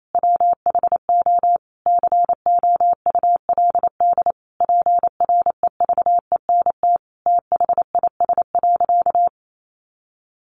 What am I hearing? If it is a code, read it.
Morse code, heard as a keyed tone, about 28 wpm, 700 Hz. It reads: WHO COULD PREVENT THIS.
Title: Sentences from Top 1100 Words 28wpm